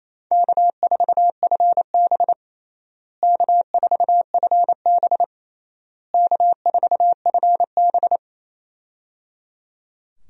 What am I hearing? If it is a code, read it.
Morse code, heard as a keyed tone, about 28 wpm, 700 Hz. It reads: K4FB K4FB K4FB
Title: USA Call Signs - 4 Characters - Repeated 3x - 28wpm